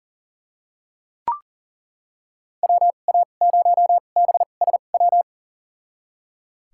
Morse code, heard as a keyed tone, about 38 wpm, spaced as for 28 wpm, 700 Hz. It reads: WA0BSW
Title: USA Call Signs - 6 Characters - 2x Character Spacing - 40wpm